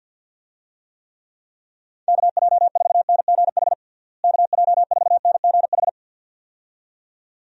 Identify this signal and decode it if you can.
Morse code, heard as a keyed tone, about 48 wpm, 700 Hz. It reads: KJ4NGH KJ4NGH
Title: USA Call Signs - 6 Characters - Repeated 2x - 50wpm